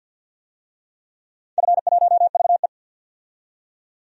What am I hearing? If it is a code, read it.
Morse code, heard as a keyed tone, about 49 wpm, 700 Hz. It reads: F1VE
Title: DXCC-100 Call Signs - 4 Characters - 50wpm